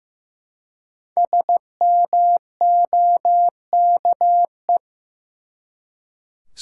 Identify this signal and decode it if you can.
Morse code, heard as a keyed tone, about 15 wpm, 700 Hz. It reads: SMOKE
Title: Five Letter Words - 15wpm